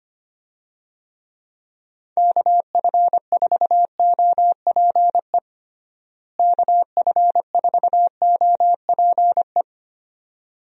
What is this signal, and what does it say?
Morse code, heard as a keyed tone, about 25 wpm, 700 Hz. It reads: KF4OPE KF4OPE
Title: USA Call Signs - 6 Characters - Repeated 2x - 25wpm